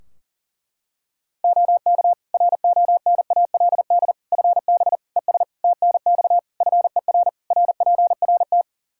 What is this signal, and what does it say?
Morse code, heard as a keyed tone, about 40 wpm, 700 Hz. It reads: OK RONALD FB ES TNX FER RPRT
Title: Practice QSO - Exchange 2 - Info - 40wpm